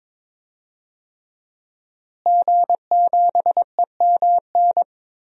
Morse code, heard as a keyed tone, about 22 wpm, 700 Hz. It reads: G7EMN
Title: DXCC-100 Call Signs - 5 Characters - Repeated 3x - 22wpm